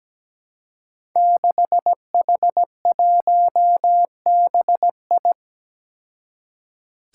Morse code, heard as a keyed tone, about 17 wpm, 700 Hz. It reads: 6H1BI
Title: Five Challenging Letter-Number Combinations 17wpm